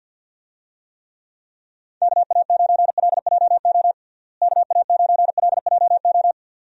Morse code, heard as a keyed tone, about 51 wpm, 700 Hz. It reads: KA9LJO KA9LJO
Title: USA Call Signs - 6 Characters - Repeated 2x - 50wpm